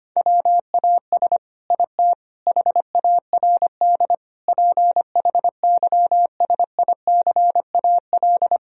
Morse code, heard as a keyed tone, about 25 wpm, 700 Hz. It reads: WAS IT HARD PHYSICAL
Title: Sentences from Top 1000 Words - Review Part 4 - 25wpm